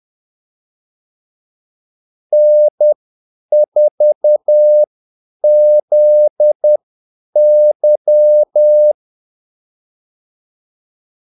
Morse code, heard as a keyed tone, about 10 wpm, spaced as for 8 wpm, 600 Hz. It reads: N4ZY
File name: Callsigns-10-8wpm